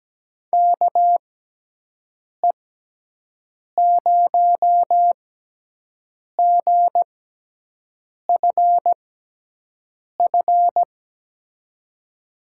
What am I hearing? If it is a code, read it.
Morse code, heard as a keyed tone, about 17 wpm, spaced as for 6 wpm, 700 Hz. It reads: KE0GFF
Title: USA Call Signs - 6x Character Spacing - 17wpm